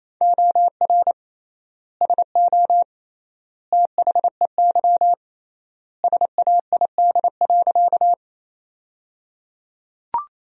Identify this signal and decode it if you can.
Morse code, heard as a keyed tone, about 28 wpm, 700 Hz. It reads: OR SO THEY SAID.
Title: 3x Word-Spacing - Sentences from Top 100 Words - 28wpm